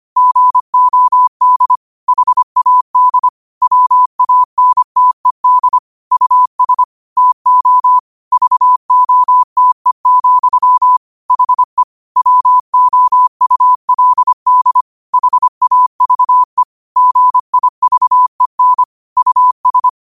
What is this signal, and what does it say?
Morse code, heard as a keyed tone, about 25 wpm, 1000 Hz. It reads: GOD HAD WANTED US TO VOTE, HE WOULD HAVE GIVEN US
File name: qotdfiles / 25wpm